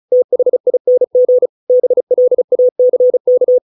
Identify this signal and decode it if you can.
Morse code, heard as a keyed tone, about 34 wpm, 500 Hz. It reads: THING BLACK